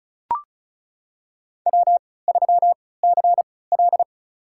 Morse code, heard as a keyed tone, about 34 wpm, spaced as for 20 wpm, 700 Hz. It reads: W3CL
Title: USA Call Signs - 4 Characters - 3x Character Spacing - 35wpm